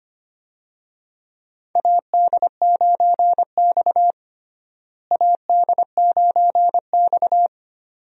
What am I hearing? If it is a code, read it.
Morse code, heard as a keyed tone, about 25 wpm, 700 Hz. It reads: AD9X AD9X
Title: USA Call Signs - 4 Characters - Repeated 2x - 25wpm